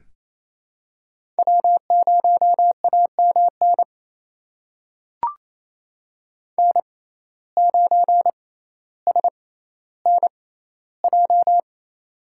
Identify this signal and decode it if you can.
Morse code, heard as a keyed tone, about 27 wpm, spaced as for 14 wpm, 700 Hz. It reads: W0AMN N9SNJ
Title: USA Call Signs - 5 Characters - 6x Character Spacing - 28wpm